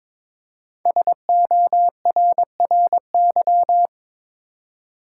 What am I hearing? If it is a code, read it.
Morse code, heard as a keyed tone, about 23 wpm, 700 Hz. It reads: SORRY
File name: 5 Letter Words - First 3 Letters Given - 22wpm